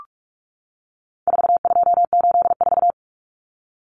Morse code, heard as a keyed tone, about 44 wpm, 700 Hz. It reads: 4284
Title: Common Cut Numbers - 4-Digit 45wpm